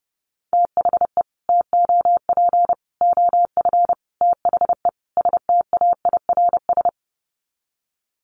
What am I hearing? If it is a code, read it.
Morse code, heard as a keyed tone, about 30 wpm, 700 Hz. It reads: THE TOP OF THE STAIRS
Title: Sets of 5 Words Encore 30wpm